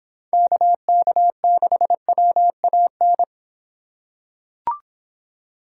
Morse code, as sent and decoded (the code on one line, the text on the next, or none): -.- -.- -.... .-- .- -.
KK6WAN